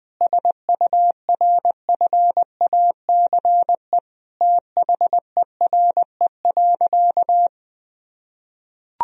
Morse code, heard as a keyed tone, about 20 wpm, 700 Hz. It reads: SURFACE THERE.
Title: Sentences from Top 1000 Words - Review Part 3 - 20wpm